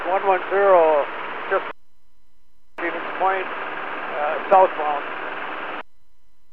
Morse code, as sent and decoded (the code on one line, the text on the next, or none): none